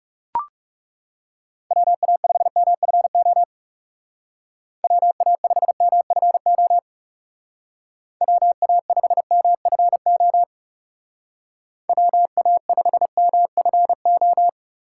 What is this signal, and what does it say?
Morse code, heard as a keyed tone, about 36 wpm, 700 Hz. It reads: WA5MFO WA5MFO WA5MFO WA5MFO
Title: Speed Racing - USA 2x3 Call Signs 30wpm